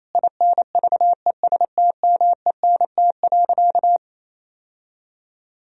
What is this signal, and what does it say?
Morse code, heard as a keyed tone, about 28 wpm, 700 Hz. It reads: INVESTMENT.
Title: Sentences from Top 1000 Words - Review Part 4 - 28wpm